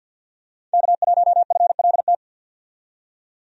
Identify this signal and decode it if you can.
Morse code, heard as a keyed tone, about 53 wpm, 700 Hz. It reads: K1FLT
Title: USA Call Signs - 5 Characters - 50wpm